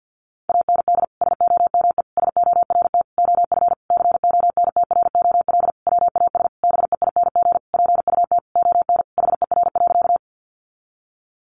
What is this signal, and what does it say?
Morse code, heard as a keyed tone, about 50 wpm, 700 Hz. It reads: AND SOME SORT OF CONTROL WAS BEING PUT ON HER.
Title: Sentences from Top 1300 Words 50wpm